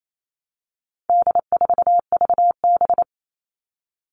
Morse code, heard as a keyed tone, about 28 wpm, 700 Hz. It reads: D4VB
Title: All Cut Numbers - 4-Digit 28wpm